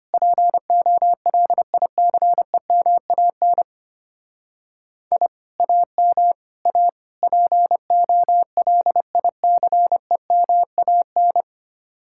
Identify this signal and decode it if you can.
Morse code, heard as a keyed tone, about 31 wpm, 700 Hz. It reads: POLICEMAN I AM A POLICEMAN
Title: Speed-Racing Common QSO Phrases Encore - 25wpm